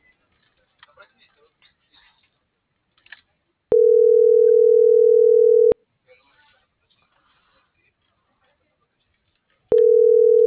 Ringback tone (North America), 440 + 480 Hz.